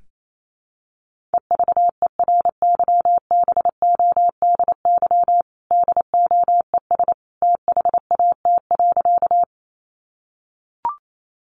Morse code, heard as a keyed tone, about 28 wpm, 700 Hz. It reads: EVERYBODY DOES THAT.
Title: Sentences from Top 1400 Words 28wpm